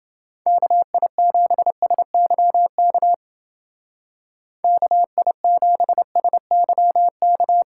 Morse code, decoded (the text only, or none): KI7SYK KI7SYK